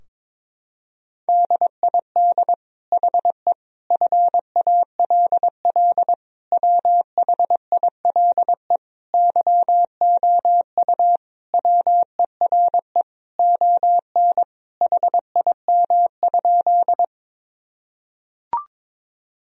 Morse code, heard as a keyed tone, about 21 wpm, 700 Hz. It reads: DID HE FALL WHILE YOU WERE ON HIM?
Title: Sentences from Top 1000 Words - Review Part 2 - 22wpm